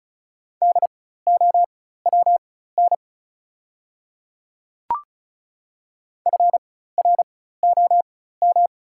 Morse code, heard as a keyed tone, about 36 wpm, spaced as for 16 wpm, 690 Hz. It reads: DOWN FROM
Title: 4x Character Spacing - Top 100 Words - 35wpm